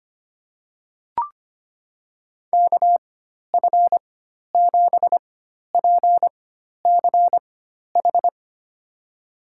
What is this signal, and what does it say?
Morse code, heard as a keyed tone, about 25 wpm, spaced as for 12 wpm, 700 Hz. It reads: KF7PCH
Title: USA Call Signs - 4x Character Spacing - 25wpm